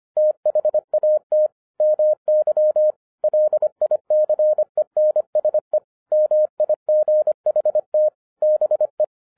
Morse code, heard as a keyed tone, about 25 wpm, 610 Hz. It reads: THAT MY LICENSE MIGHT BE
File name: 2012-01-mp3 / n08-ITS MY FAULT, BUT-25wpm